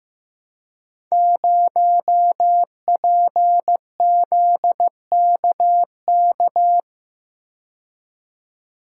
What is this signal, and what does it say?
Morse code, heard as a keyed tone, about 15 wpm, 700 Hz. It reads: 0PZKK
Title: Code Groups - 5 Letter:Numbers - 15wpm